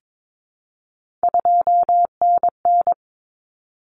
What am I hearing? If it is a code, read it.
Morse code, heard as a keyed tone, about 22 wpm, 700 Hz. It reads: 2NN
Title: Common Cut Numbers - 3-Digit 22wpm